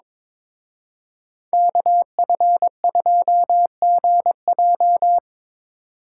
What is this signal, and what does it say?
Morse code, heard as a keyed tone, about 26 wpm, 700 Hz. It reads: KF2GJ
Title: Speed Racing - USA Call Signs 22wpm